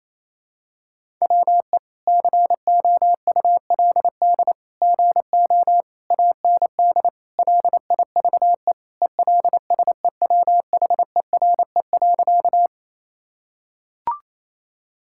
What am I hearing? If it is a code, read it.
Morse code, heard as a keyed tone, about 28 wpm, 700 Hz. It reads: WE COULD GO AND LIVE ELSEWHERE.
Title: Sentences from Top 1600 Words 28wpm